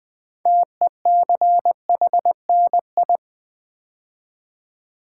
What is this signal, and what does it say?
Morse code, heard as a keyed tone, about 20 wpm, 700 Hz. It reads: TECHNI
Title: Build A Word - 3 Letters At A Time - 9-Letter Words - 20wpm